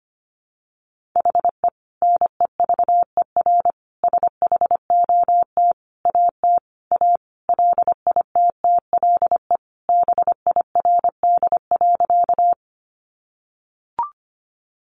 Morse code, heard as a keyed tone, about 25 wpm, 700 Hz. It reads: HE NEVER SHOT AT A LITTLE BIRD.
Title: Sentences from Top 1100 Words 25wpm